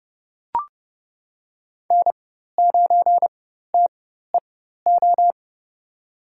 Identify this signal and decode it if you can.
Morse code, heard as a keyed tone, about 29 wpm, spaced as for 14 wpm, 700 Hz. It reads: N9TEO
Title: USA Call Signs - 4x Character Spacing - 30wpm